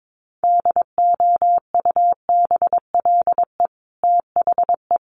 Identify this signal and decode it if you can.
Morse code, heard as a keyed tone, about 22 wpm, 700 Hz. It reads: DOUBLE THE
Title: Sentences from Top 1400 Words 22wpm